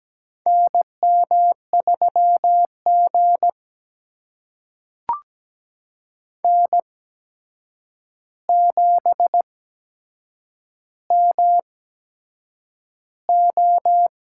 Morse code, heard as a keyed tone, about 17 wpm, spaced as for 7 wpm, 700 Hz. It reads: NM3G N7MO
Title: USA Call Signs - 4 Characters - 8x Character Spacing - 17wpm